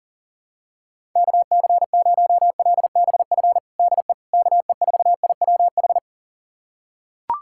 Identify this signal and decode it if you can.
Morse code, heard as a keyed tone, about 40 wpm, 700 Hz. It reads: KC0LBF DE KE4IWH
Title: USA Callsign DE Callsign - 6 Characters - 40wpm